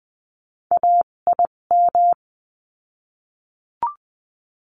Morse code, heard as a keyed tone, about 20 wpm, spaced as for 17 wpm, 700 Hz. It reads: AIM
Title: 257 Lesson 42b - Words 14-20wpm